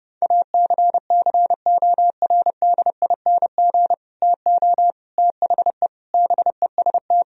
Morse code, heard as a keyed tone, about 30 wpm, 700 Hz. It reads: ACCORDING TO THE BEST
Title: Sentences from Top 1000 Words - Review Part 4 - 30wpm